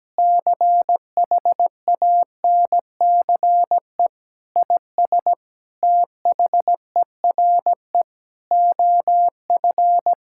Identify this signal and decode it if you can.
Morse code, heard as a keyed tone, about 17 wpm, 700 Hz. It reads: CHANCE IS THERE OF
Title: Sentences from Top 1000 Words - Review Part 6 - 17wpm